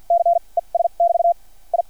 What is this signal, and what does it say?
Morse code, heard as a keyed tone, about 50 wpm, spaced as for 37 wpm, 690 Hz. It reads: C ES<BT> I
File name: archived / CRSnet